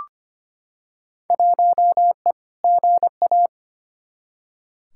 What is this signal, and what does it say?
Morse code, heard as a keyed tone, about 25 wpm, 700 Hz. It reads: 1E GA
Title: Field Day Exchanges - 25wpm